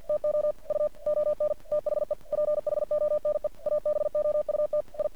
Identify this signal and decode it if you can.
Morse code, heard as a keyed tone, about 50 wpm, 600 Hz. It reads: TO U ON THE PHONE ABOUT I
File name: archived / CRSnet